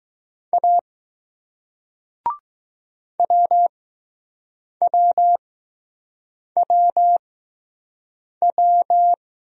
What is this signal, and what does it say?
Morse code, heard as a keyed tone, about 20 wpm, spaced as for 6 wpm, 700 Hz. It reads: A WWWW